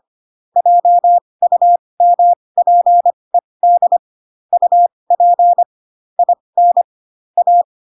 Morse code, heard as a keyed tone, about 25 wpm, spaced as for 20 wpm, 700 Hz. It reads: JUMPED UP IN A
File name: Pinocchio / Pinocchio ch4 15 25 ik1hge